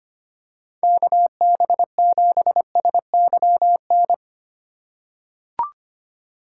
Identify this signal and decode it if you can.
Morse code, heard as a keyed tone, about 25 wpm, 700 Hz. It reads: KB7SYN
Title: USA Call Signs - 6 Characters - 6x Character Spacing - 25wpm